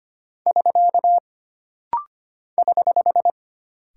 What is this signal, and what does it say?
Morse code, heard as a keyed tone, about 25 wpm, 700 Hz. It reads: <SK> <HH>
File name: Single Punctuation-Prosign - Rapid-Fire - 25wpm